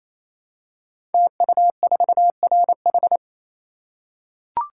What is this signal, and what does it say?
Morse code, heard as a keyed tone, about 27 wpm, 700 Hz. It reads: TU4RH
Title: Code Groups - 5 Letter:Numbers - 28wpm